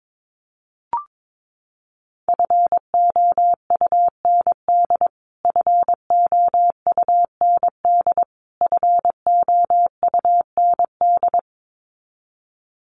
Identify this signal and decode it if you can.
Morse code, heard as a keyed tone, about 22 wpm, 700 Hz. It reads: FOUND FOUND FOUND
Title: Top 500 Words Repeated 22wpm